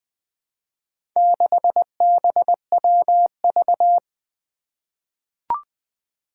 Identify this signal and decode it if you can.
Morse code, heard as a keyed tone, about 20 wpm, 700 Hz. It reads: 6BWV
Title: Four Challenging Letter-Number Combinations 20wpm